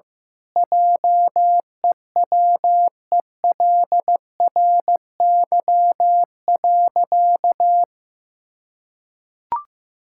Morse code, heard as a keyed tone, about 15 wpm, 700 Hz. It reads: JEWELRY.
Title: Pangram Challenge v2 - 15wpm